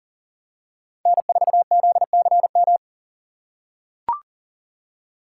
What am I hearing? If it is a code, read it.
Morse code, heard as a keyed tone, about 40 wpm, 700 Hz. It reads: N4ZCM
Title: USA Call Signs - 5 Characters - 3x Character Spacing - 40wpm